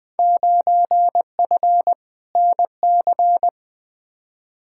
Morse code, heard as a keyed tone, about 20 wpm, 700 Hz. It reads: 9F NC
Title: Field Day Exchanges Encore - 20wpm